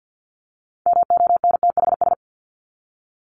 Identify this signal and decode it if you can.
Morse code, heard as a keyed tone, about 50 wpm, 700 Hz. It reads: MONTHS
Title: ES:S Suffix - Top 100 Words with Base Words 50wpm